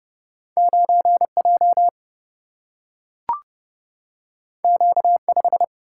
Morse code, heard as a keyed tone, about 31 wpm, 700 Hz. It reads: 9 J Q 5
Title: Single-Letter Single-Number 30wpm